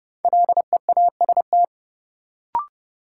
Morse code, heard as a keyed tone, about 30 wpm, 700 Hz. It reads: LEAST
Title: Sentences from Top 700 Words - New Word Drill - 30wpm